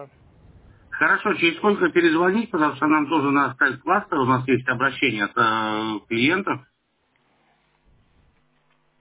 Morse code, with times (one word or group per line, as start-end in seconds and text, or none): none